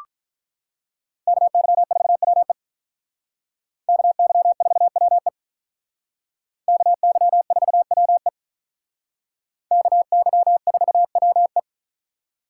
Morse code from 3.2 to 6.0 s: KY4WE

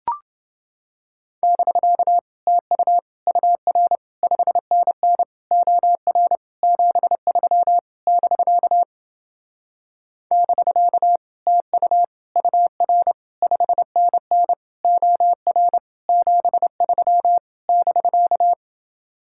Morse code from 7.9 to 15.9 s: <BK> <BK> TU UR 5NN OR

